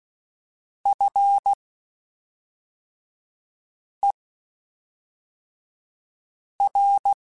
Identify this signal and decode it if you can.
Morse code, heard as a keyed tone, about 16 wpm, spaced as for 3 wpm, 800 Hz. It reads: FER
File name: (spaced) / QSOs Spaced-mp3s / kc7sac @5 sp